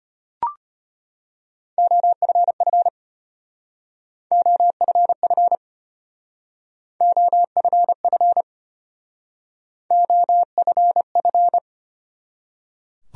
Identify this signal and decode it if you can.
Morse code, heard as a keyed tone, about 30 wpm, 700 Hz. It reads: OFF OFF OFF OFF
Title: Speed-Racing Top 200 Words 25wpm